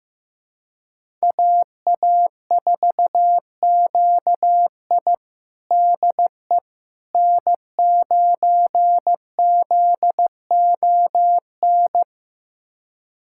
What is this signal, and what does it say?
Morse code, heard as a keyed tone, about 15 wpm, 700 Hz. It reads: AA4QI DE N9ZON
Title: USA Callsign DE Callsign - 5 Characters - 15wpm